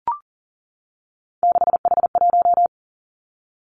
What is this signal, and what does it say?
Morse code, heard as a keyed tone, about 38 wpm, 700 Hz. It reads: 6H1
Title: Three Challenging Letter-Number Combinations 40wpm